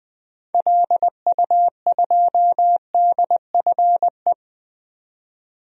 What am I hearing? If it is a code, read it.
Morse code, heard as a keyed tone, about 20 wpm, 700 Hz. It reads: LU2DFE
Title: DXCC-100 Call Signs - 6 Characters - 20wpm